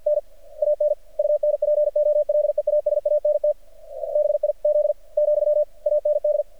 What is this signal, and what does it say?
Morse code, heard as a keyed tone, about 48 wpm, 600 Hz. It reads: N AN UNPOLEASANT BIZ , AND
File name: archived / CRSnet